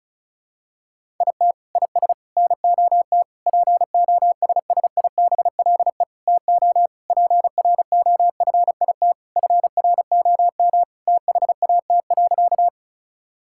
Morse code, heard as a keyed tone, about 34 wpm, 700 Hz. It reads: IT IS NOT POSSIBLE TO PROFIT FROM THAT.
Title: Sentences from Top 1000 Words - Review Part 5 - 35wpm